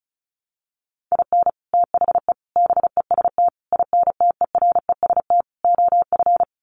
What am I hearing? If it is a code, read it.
Morse code, heard as a keyed tone, about 34 wpm, 700 Hz. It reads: IN THE BEST INTEREST OF
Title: Sets of 5 Words Encore 35wpm